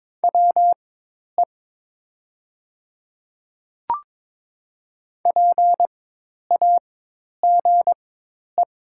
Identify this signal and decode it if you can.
Morse code, heard as a keyed tone, about 22 wpm, spaced as for 10 wpm, 700 Hz. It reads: WE PAGE
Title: 4x Character Spacing - Top 500 Words - 22wpm